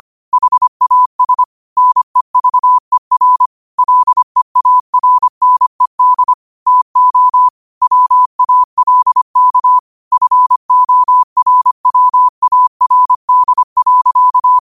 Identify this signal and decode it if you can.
Morse code, heard as a keyed tone, about 25 wpm, 1000 Hz. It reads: HAS NEVER LEARNED TO WALK FORWARD.
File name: qotdfiles / 25wpm